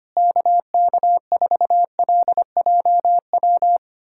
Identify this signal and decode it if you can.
Morse code, heard as a keyed tone, about 25 wpm, 700 Hz. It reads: KK4LJW
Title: USA Callsign DE Callsign 25wpm